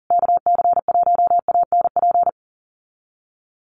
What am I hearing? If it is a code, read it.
Morse code, heard as a keyed tone, about 40 wpm, 700 Hz. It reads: KC1ANP
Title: USA Call Signs Repeated 2x 40wpm